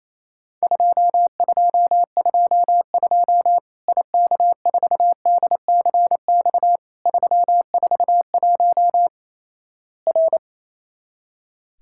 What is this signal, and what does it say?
Morse code, heard as a keyed tone, about 28 wpm, 700 Hz. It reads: 2222 IK4DCX 341 R
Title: Worked All Europe DX Contest - QTC Exchange - 28wpm